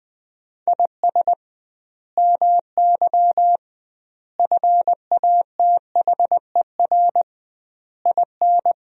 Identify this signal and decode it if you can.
Morse code, heard as a keyed tone, about 20 wpm, 700 Hz. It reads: IS MY FATHER IN
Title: 2x Word-Spacing - Sentences from Top 500 Words - 20wpm